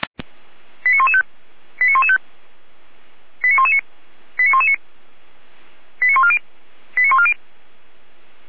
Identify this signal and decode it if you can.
Five-tone selective call (ZVEI), heard as quick runs of five tones, tones of about 80 ms. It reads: 89115 89115 89119 89119 89140 89140